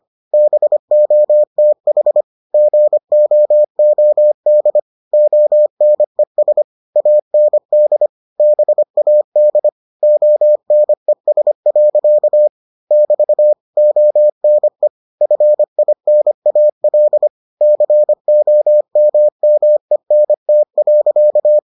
Morse code, heard as a keyed tone, about 25 wpm, 600 Hz. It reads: BOTH GOOD ONES AND BAD ONES. <BT> ONE FINAL COMMENT.